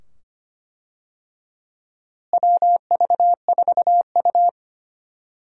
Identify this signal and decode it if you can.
Morse code, heard as a keyed tone, about 25 wpm, 700 Hz. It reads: WV4U
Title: USA Call Signs - 4 Characters - Repeated 2x - 25wpm